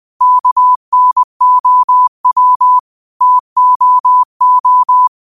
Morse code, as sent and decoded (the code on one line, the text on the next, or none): -.- -. --- .-- / - --- ---
KNOW TOO